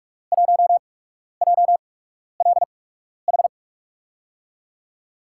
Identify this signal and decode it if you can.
Morse code, heard as a keyed tone, about 46 wpm, spaced as for 12 wpm, 700 Hz. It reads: 1JLH
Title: USA Call Signs - 6 Characters - 8x Character Spacing - 45wpm